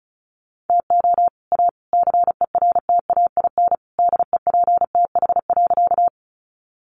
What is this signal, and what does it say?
Morse code, heard as a keyed tone, about 34 wpm, 700 Hz. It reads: TO A CERTAIN DEPTH.
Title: Sentences from Top 1200 Words 35wpm